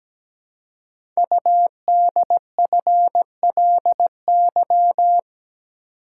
Morse code, heard as a keyed tone, about 17 wpm, 700 Hz. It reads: UDFLY
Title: Code Groups - 5 Letters - 17wpm